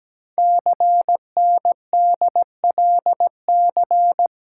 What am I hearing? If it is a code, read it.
Morse code, heard as a keyed tone, about 17 wpm, 700 Hz. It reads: CNDLC
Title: Code Groups - 5 Letters - 17wpm